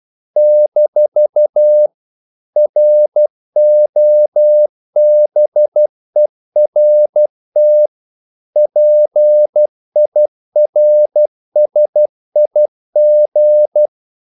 Morse code, heard as a keyed tone, about 12 wpm, 600 Hz. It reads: - ROBERT PIRSIG
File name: CW / Practice / EnglishQuotations0112